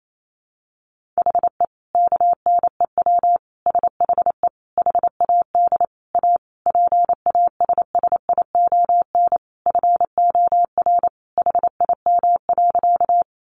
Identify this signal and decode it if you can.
Morse code, heard as a keyed tone, about 28 wpm, 700 Hz. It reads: HE KNEW SHE HAD A PASSION FOR HIM.
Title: Sentences from Top 1300 Words 28wpm